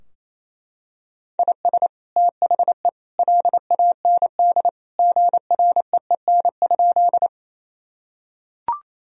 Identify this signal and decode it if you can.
Morse code, heard as a keyed tone, about 28 wpm, 700 Hz. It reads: IS THE LAND GREEN?
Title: Sentences from Top 1000 Words - Review Part 2 - 28wpm